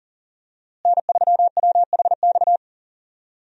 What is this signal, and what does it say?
Morse code, heard as a keyed tone, about 40 wpm, 700 Hz. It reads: N3WHX
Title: USA Call Signs - 5 Characters - Repeated 3x - 40wpm